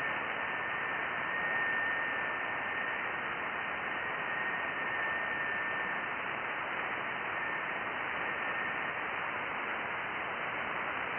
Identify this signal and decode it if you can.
FT8.
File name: IC-7100